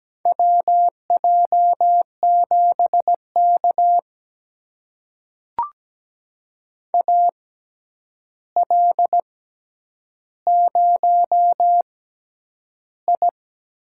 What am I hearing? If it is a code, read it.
Morse code, heard as a keyed tone, about 17 wpm, spaced as for 9 wpm, 700 Hz. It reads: WJ7K AL0I
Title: USA Call Signs - 4 Characters - 6x Character Spacing - 17wpm